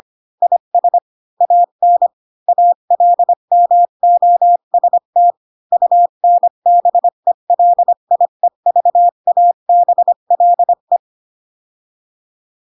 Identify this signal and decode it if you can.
Morse code, heard as a keyed tone, about 25 wpm, 700 Hz. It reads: IS AN ALMOST UNBELIEVABLE
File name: Pinocchio / Pinocchio ch3 20 25 ik1hge